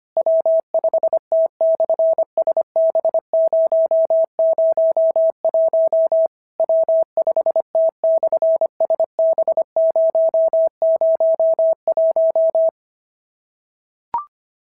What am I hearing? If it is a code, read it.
Morse code, heard as a keyed tone, about 25 wpm, 650 Hz. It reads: W5T/SB001 W5T/SB001
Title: SOTA QSO Practice 25wpm